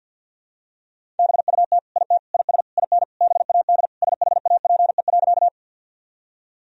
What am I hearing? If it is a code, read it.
Morse code, heard as a keyed tone, about 47 wpm, 700 Hz. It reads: BUT IT IS IN BAD SHAPE.
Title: Sentences from Top 1000 Words - Review Part 3 - 50wpm